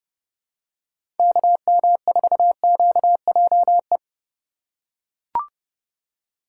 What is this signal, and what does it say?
Morse code, heard as a keyed tone, about 31 wpm, 700 Hz. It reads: KM4QJE
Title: USA Call Signs - 6 Characters - 3x Character Spacing - 30wpm